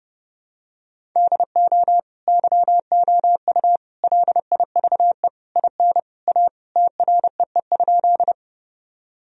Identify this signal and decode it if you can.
Morse code, heard as a keyed tone, about 30 wpm, 700 Hz. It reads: DO YOU LIVE IN A TREE?
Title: Sentences from Top 1000 Words - Review Part 1 - 30wpm